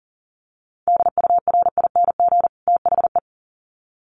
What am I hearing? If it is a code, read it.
Morse code, heard as a keyed tone, about 40 wpm, 700 Hz. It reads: DURING THE
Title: Sets of 2 Words 40wpm